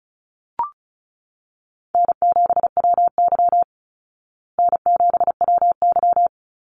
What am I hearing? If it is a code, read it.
Morse code, heard as a keyed tone, about 34 wpm, 700 Hz. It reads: N7WY N7WY
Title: Easy Call Signs Repeated 2x - 35wpm